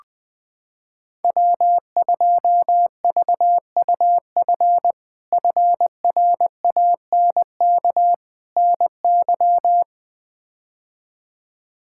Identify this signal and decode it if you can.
Morse code, heard as a keyed tone, about 20 wpm, 700 Hz. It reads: W2VUF FRANK NY
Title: K1USN SST QSO Practice - 20wpm